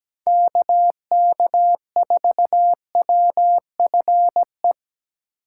Morse code, heard as a keyed tone, about 17 wpm, 700 Hz. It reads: KK4WFE
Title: USA Call Signs - 3x Character Spacing - 17wpm